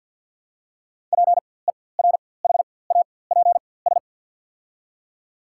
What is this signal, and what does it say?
Morse code, heard as a keyed tone, about 53 wpm, spaced as for 24 wpm, 700 Hz. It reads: PERHAPS
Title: 4x Character Spacing - Top 500 Words - 50wpm